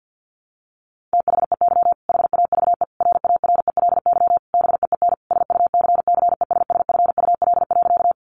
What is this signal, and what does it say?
Morse code, heard as a keyed tone, about 50 wpm, 700 Hz. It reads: THEY HAVE RARELY BEEN SUCCESSFUL.
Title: Sentences from Top 1400 Words 50wpm